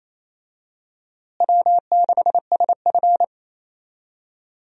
Morse code, heard as a keyed tone, about 28 wpm, 700 Hz. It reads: W6SF
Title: USA Call Signs - 4 Characters - Repeated 3x - 28wpm